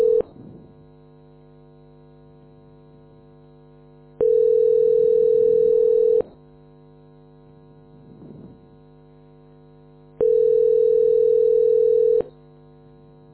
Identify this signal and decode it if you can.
Ringback tone (North America), 440 + 480 Hz.
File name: f80dc860-67e6-4459-8175